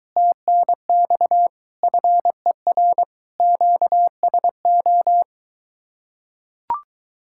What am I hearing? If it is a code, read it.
Morse code, heard as a keyed tone, about 22 wpm, 700 Hz. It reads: TNX FER QSO